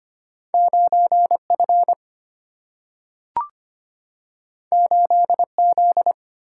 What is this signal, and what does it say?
Morse code, heard as a keyed tone, about 25 wpm, 700 Hz. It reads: 9 F 8 Z
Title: Single-Letter Single-Number 25wpm